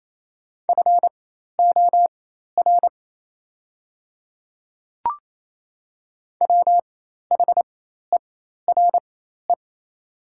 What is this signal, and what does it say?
Morse code, heard as a keyed tone, about 28 wpm, spaced as for 13 wpm, 690 Hz. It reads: FOR WHERE
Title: 4x Character Spacing - Top 200 Words - 28wpm